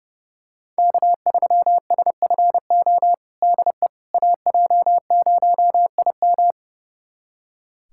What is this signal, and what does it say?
Morse code, heard as a keyed tone, about 30 wpm, 700 Hz. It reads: K3SFO DE AJ0IM
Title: USA Callsign DE Callsign - 5 Characters - 30wpm